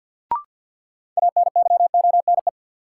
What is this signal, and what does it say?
Morse code, heard as a keyed tone, about 49 wpm, 700 Hz. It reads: ANYONE
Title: Sentences from Top 900 Words - New Word Drill - 50wpm